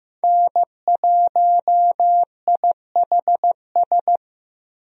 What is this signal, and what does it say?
Morse code, heard as a keyed tone, about 15 wpm, 700 Hz. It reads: N1IHS
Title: USA Call Signs - 5 Characters - Repeated 3x - 15wpm